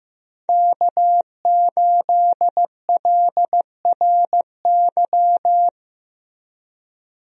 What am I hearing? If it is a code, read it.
Morse code, heard as a keyed tone, about 15 wpm, 700 Hz. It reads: K8LRY
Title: USA Call Signs - 5 Characters - Repeated 3x - 15wpm